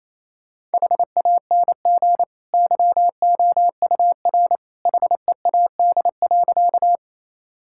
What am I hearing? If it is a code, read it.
Morse code, heard as a keyed tone, about 28 wpm, 700 Hz. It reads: HANG YOUR HEAD.
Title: Sentences from Top 1600 Words 28wpm